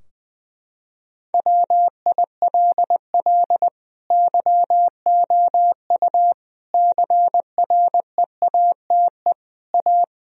Morse code, heard as a keyed tone, about 20 wpm, 700 Hz. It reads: WILL YOU CREATE A
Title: Sentences from Top 1000 Words - Review Part 3 - 20wpm